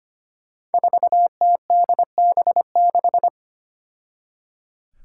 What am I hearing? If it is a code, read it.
Morse code, heard as a keyed tone, about 25 wpm, 700 Hz. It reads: 4TDB6
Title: Five Challenging Letter-Number Combinations 25wpm